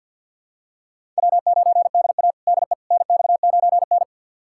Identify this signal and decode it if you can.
Morse code, heard as a keyed tone, about 47 wpm, 700 Hz. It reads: W9DA DE NX9N
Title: USA Callsign DE Callsign - 4 Characters - 50wpm